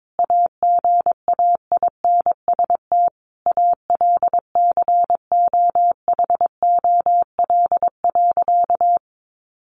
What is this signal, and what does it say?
Morse code, heard as a keyed tone, about 22 wpm, 700 Hz. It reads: AGAINST ALCOHOL.
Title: Sentences from Top 1400 Words 22wpm